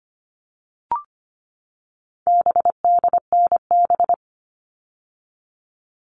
Morse code, heard as a keyed tone, about 25 wpm, 700 Hz. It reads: BDNB
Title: All Cut Numbers - 4-Digit 25wpm